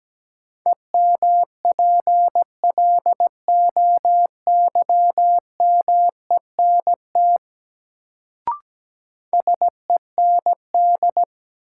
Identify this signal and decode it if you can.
Morse code, heard as a keyed tone, about 17 wpm, 700 Hz. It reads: EMPLOYMENT SEND
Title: Sentences from Top 1500 Words - New Word Drill Alt - 17wpm